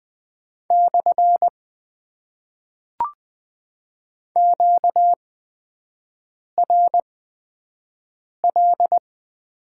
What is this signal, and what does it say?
Morse code, heard as a keyed tone, about 20 wpm, spaced as for 5 wpm, 700 Hz. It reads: / QRL
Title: Top 90 QSO Elements - 8x Character Spacing - 20wpm